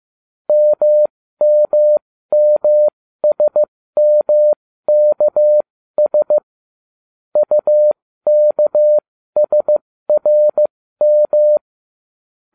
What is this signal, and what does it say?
Morse code, heard as a keyed tone, about 15 wpm, spaced as for 13 wpm, 600 Hz. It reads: MMMSMKS UKSRM